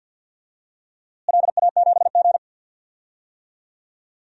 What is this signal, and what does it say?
Morse code, heard as a keyed tone, about 52 wpm, 700 Hz. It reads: LA7G
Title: DXCC-100 Call Signs - 4 Characters - Repeated 2x - 50wpm